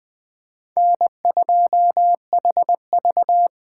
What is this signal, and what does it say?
Morse code, heard as a keyed tone, about 24 wpm, 700 Hz. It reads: N2HV
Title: Speed Racing - USA 1x2 Call Signs 20wpm